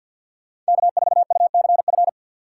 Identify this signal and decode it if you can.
Morse code, heard as a keyed tone, about 62 wpm, 700 Hz. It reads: K4UCF
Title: Speed Racing - USA 1x3 Call Signs 50wpm